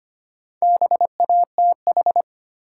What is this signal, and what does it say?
Morse code, heard as a keyed tone, about 25 wpm, 700 Hz. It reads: BATH
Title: Sentences from Top 1600 Words - New Word Drill Alt - 25wpm